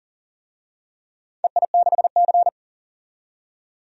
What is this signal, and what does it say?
Morse code, heard as a keyed tone, about 40 wpm, 700 Hz. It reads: EI6C
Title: DXCC-100 Call Signs - 4 Characters - 40wpm